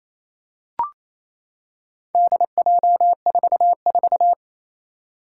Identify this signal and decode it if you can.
Morse code, heard as a keyed tone, about 29 wpm, 700 Hz. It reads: DJ44
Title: Four Challenging Letter-Number Combinations 28wpm